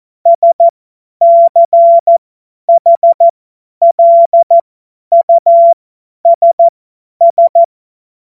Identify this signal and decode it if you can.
Morse code, heard as a keyed tone, about 14 wpm, spaced as for 10 wpm, 680 Hz. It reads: SCHLUSS
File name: Morsekurs / Witze / Witz 24. Januar 2023 50 F70